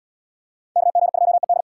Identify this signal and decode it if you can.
Morse code, heard as a keyed tone, about 67 wpm, 700 Hz. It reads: KG2ED